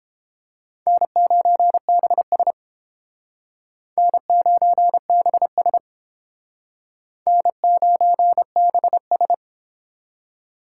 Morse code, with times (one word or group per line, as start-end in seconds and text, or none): 0.9-2.5 N9BS
4.0-5.8 N9BS
7.3-9.3 N9BS